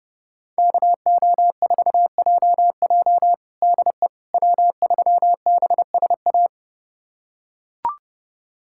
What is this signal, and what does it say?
Morse code, heard as a keyed tone, about 31 wpm, 700 Hz. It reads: KO4JJ DE W3BSA
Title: USA Callsign DE Callsign - 5 Characters - 30wpm